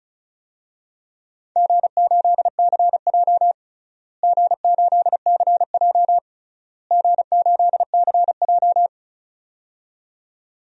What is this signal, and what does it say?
Morse code, heard as a keyed tone, about 34 wpm, 700 Hz. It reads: G8CJ G8CJ G8CJ
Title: DXCC-100 Call Signs - 4 Characters - Repeated 3x - 35wpm